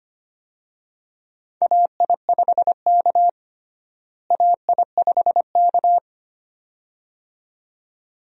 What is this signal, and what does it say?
Morse code, heard as a keyed tone, about 25 wpm, 700 Hz. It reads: AI5K AI5K
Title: USA Call Signs - 4 Characters - Repeated 2x - 25wpm